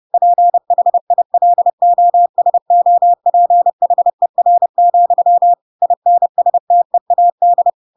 Morse code, heard as a keyed tone, about 30 wpm, 700 Hz. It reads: PHILOSOPHER, INSTEAD
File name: Pinocchio / Pinocchio ch4 30 30 ik1hge